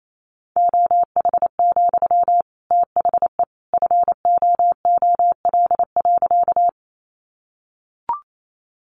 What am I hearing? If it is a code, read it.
Morse code, heard as a keyed tone, about 28 wpm, 700 Hz. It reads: OH, THE FOOL.
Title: A Sound Of Thunder 28wpm